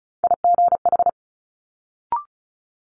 Morse code, heard as a keyed tone, about 36 wpm, 700 Hz. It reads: IGH
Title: Multi-letter Phonemes Quick Fire 35wpm